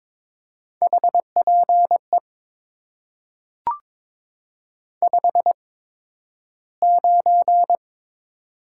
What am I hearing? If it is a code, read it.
Morse code, heard as a keyed tone, about 22 wpm, spaced as for 12 wpm, 700 Hz. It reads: HPE 59
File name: Top 90 QSO Elements - 8x Character Spacing - 22wpm